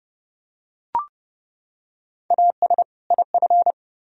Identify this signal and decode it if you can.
Morse code, heard as a keyed tone, about 30 wpm, 700 Hz. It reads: AS IF